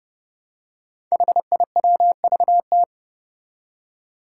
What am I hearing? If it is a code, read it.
Morse code, heard as a keyed tone, about 30 wpm, 700 Hz. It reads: HIWVT
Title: Five Challenging Letter-Number Combinations 30wpm